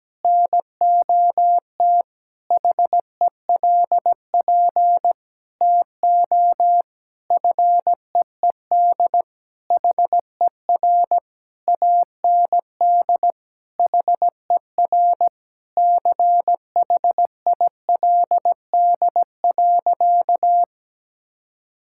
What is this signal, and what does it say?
Morse code, heard as a keyed tone, about 17 wpm, 700 Hz. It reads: NOT HELP TO FEED HER AND HER CHILD.
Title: Sentences from Top 1600 Words 17wpm